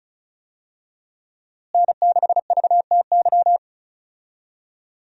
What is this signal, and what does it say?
Morse code, heard as a keyed tone, about 34 wpm, 700 Hz. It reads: N6VTY
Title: USA Call Signs - 5 Characters - Repeated 3x - 35wpm